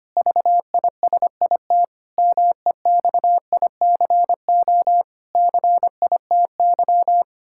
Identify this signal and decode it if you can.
Morse code, heard as a keyed tone, about 25 wpm, 700 Hz. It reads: VISIT MEXICO CITY
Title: Pangram Challenge v2 - 25wpm